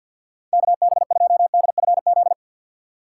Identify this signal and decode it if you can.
Morse code, heard as a keyed tone, about 52 wpm, 700 Hz. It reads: KB2DFZ
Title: USA Call Signs - 4x Character Spacing - 50wpm